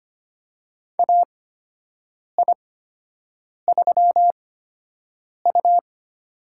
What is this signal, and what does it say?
Morse code, heard as a keyed tone, about 25 wpm, spaced as for 7 wpm, 700 Hz. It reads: AI3U
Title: USA Call Signs - 4 Characters - 8x Character Spacing - 25wpm